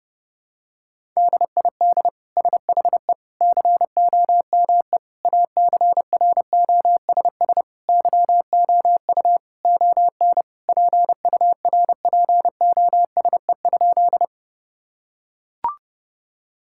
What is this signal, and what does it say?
Morse code, heard as a keyed tone, about 30 wpm, 700 Hz. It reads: DID SHE COME ACROSS YOU ON PURPOSE?
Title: Sentences from Top 1000 Words - Review Part 5 - 30wpm